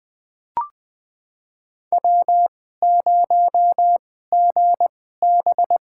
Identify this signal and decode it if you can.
Morse code, heard as a keyed tone, about 21 wpm, spaced as for 15 wpm, 700 Hz. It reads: W0GB
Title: USA Call Signs - 2x Character Spacing - 20wpm